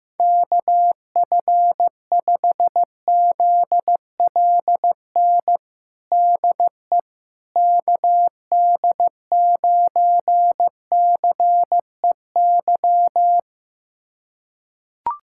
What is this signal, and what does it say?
Morse code, heard as a keyed tone, about 16 wpm, 700 Hz. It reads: KF5ZLN DE KD9CEY
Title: USA Callsign DE Callsign 15wpm